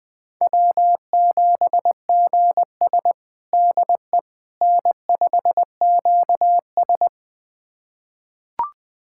Morse code, heard as a keyed tone, about 20 wpm, 700 Hz. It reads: W7GS DE N5QS
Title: USA Callsign DE Callsign - 4 Characters - 20wpm